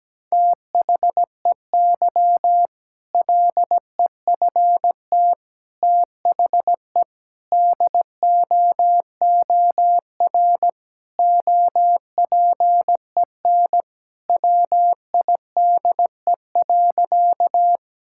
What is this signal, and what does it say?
Morse code, heard as a keyed tone, about 17 wpm, 700 Hz. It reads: THEY LEFT THE DOOR OPEN WIDE.
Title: Sentences from Top 1000 Words - Review Part 6 - 17wpm